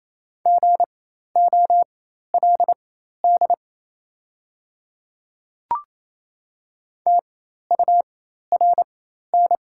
Morse code, heard as a keyed tone, about 28 wpm, spaced as for 13 wpm, 690 Hz. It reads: GOLD TURN
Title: 4x Character Spacing - Top 500 Words - 28wpm